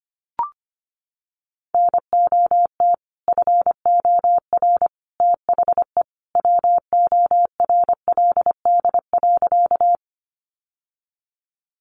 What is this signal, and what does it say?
Morse code, heard as a keyed tone, about 25 wpm, 700 Hz. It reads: NOT FOR THE WORLD.
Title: Sentences from Top 600 Words 25wpm